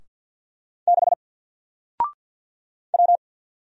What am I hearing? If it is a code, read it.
Morse code, heard as a keyed tone, about 48 wpm, 700 Hz. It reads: 6 W
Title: Single Letter-Number Alt 50wpm